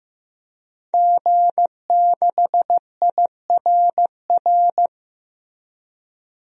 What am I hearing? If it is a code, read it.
Morse code, heard as a keyed tone, about 15 wpm, 700 Hz. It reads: G6IRR
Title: DXCC-100 Call Signs - 5 Characters - Repeated 3x - 15wpm